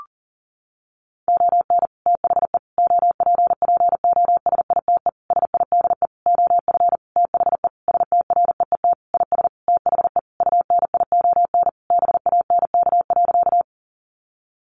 Morse code, heard as a keyed tone, about 40 wpm, 700 Hz. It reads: ON THE OPPOSITE SIDE OF THE STREET IS THE UNION BANK.
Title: Sentences from Top 1400 Words 40wpm